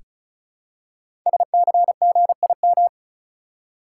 Morse code, heard as a keyed tone, about 36 wpm, 700 Hz. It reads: SCGIM
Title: Code Groups - 5 Letter:Numbers - 35wpm